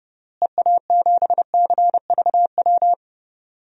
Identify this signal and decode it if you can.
Morse code, heard as a keyed tone, about 32 wpm, 700 Hz. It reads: EA7CVW
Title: DXCC-100 Call Signs - 6 Characters - Repeated 3x - 30wpm